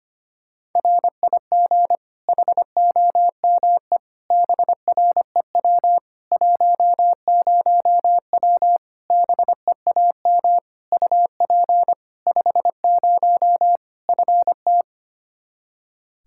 Morse code, heard as a keyed tone, about 25 wpm, 700 Hz. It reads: RIG HOME BREW 10W BEAM UP 50 FT
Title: Practice QSO - Exchange 2 - Info - 25wpm